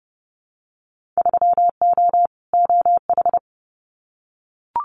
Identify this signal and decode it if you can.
Morse code, heard as a keyed tone, about 29 wpm, 700 Hz. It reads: 3O OH
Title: Winter Field Day Exchanges 30wpm